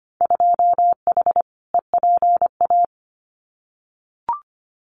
Morse code, heard as a keyed tone, about 25 wpm, 700 Hz. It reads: 2H EPA
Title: Winter Field Day Exchanges Encore 25wpm